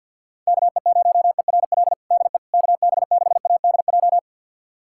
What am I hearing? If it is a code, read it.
Morse code, heard as a keyed tone, about 50 wpm, 700 Hz. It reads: KE0ERL DE KB6ADJ
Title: USA Callsign DE Callsign 50wpm